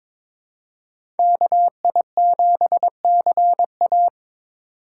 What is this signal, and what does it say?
Morse code, heard as a keyed tone, about 26 wpm, 700 Hz. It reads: KI7CA
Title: Speed Racing - USA Call Signs 22wpm